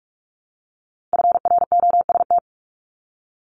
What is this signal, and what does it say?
Morse code, heard as a keyed tone, about 46 wpm, 700 Hz. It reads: FROST
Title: Multi-letter Phonemes Repeated 3x 45wpm